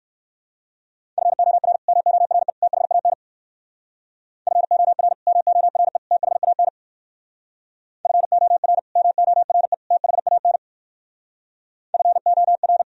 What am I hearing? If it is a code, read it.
Morse code, heard as a keyed tone, about 58 wpm, 700 Hz. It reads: FOR MORE THAN FOR MORE THAN FOR MORE THAN FOR